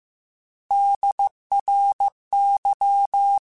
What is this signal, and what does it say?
Morse code, heard as a keyed tone, about 15 wpm, 800 Hz. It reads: DRY
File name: Alphabetical / Alphabetical mp3s / Ds @13